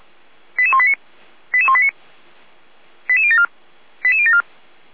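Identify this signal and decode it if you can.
Five-tone selective call (ZVEI), heard as quick runs of five tones, tones of about 70 ms. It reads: 88189 88189 88074 88074
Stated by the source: Received on an SDR as a ZVEI five-tone call, code 88505.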